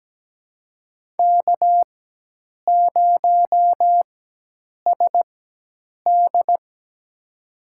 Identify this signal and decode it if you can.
Morse code, heard as a keyed tone, about 17 wpm, spaced as for 8 wpm, 700 Hz. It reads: K0SD
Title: USA Call Signs - 4x Character Spacing - 17wpm